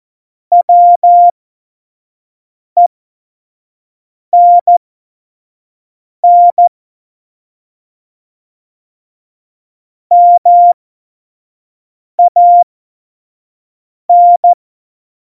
Morse code, heard as a keyed tone, about 14 wpm, spaced as for 5 wpm, 700 Hz. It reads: WENN MAN
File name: Morsekurs / Witze / Witz 19 Mai 2020, 25 F70